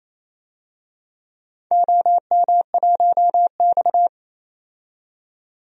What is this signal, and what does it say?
Morse code, heard as a keyed tone, about 27 wpm, 700 Hz. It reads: OM1X
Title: DXCC-100 Call Signs - 4 Characters - 28wpm